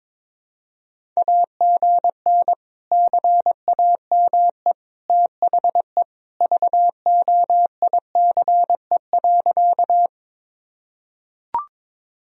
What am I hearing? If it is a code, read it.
Morse code, heard as a keyed tone, about 21 wpm, 700 Hz. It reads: AGN CAME THE VOICE.
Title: Sentences from Top 500 Words and Abbreviations 22wpm